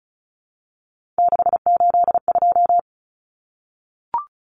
Morse code, heard as a keyed tone, about 34 wpm, 700 Hz. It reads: 682
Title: Common Cut Numbers - 3-Digit 35wpm